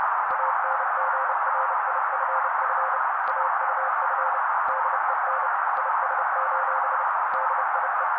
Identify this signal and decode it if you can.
Morse code, heard as a keyed tone, about 30 wpm, 530 Hz. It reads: ANGRIFF AUF DEN IS: DIE
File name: daily / 29wpm noise